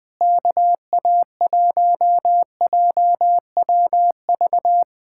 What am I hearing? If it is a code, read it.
Morse code, heard as a keyed tone, about 20 wpm, 700 Hz. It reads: KA1JWV
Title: Speed-Racing Common QSO Phrases - 20wpm